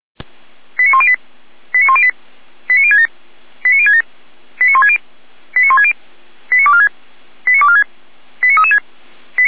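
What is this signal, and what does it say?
Five-tone selective call (ZVEI), heard as quick runs of five tones, tones of about 70 ms. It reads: 89108 89108 89067 89067 89160 89160 89246 89246 89336
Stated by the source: ZVEI five-tone selective call 89099, received by SDR.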